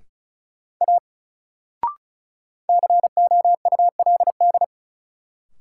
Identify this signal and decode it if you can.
Morse code, heard as a keyed tone, about 36 wpm, 700 Hz. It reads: A COULD
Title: Sentences from Top 100 Words - New Word Drill - 35wpm